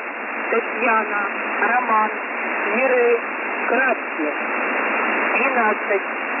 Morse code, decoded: none